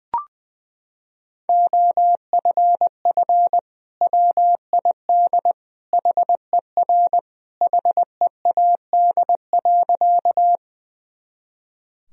Morse code, heard as a keyed tone, about 21 wpm, 700 Hz. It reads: OFF WID HER HEAD.
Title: Sentences from Top 300 Words and Abbreviations 20wpm